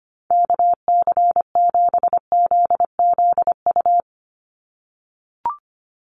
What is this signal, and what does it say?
Morse code, heard as a keyed tone, about 25 wpm, 700 Hz. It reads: KC7ZZU
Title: USA Call Signs 25wpm